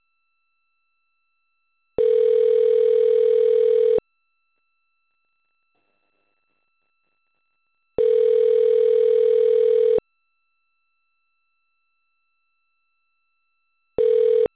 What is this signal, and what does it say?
Ringback tone (North America), 440 + 480 Hz.